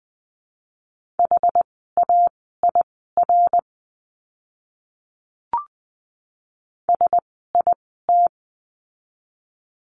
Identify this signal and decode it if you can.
Morse code, heard as a keyed tone, about 20 wpm, spaced as for 14 wpm, 700 Hz. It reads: HAIR SIT
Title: 101 Lesson 17f - Letter u - Words 20wpm